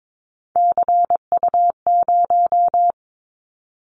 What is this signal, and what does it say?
Morse code, heard as a keyed tone, about 22 wpm, 700 Hz. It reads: CU0
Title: Callsign Prefixes for Top 100 DXCC - 22wpm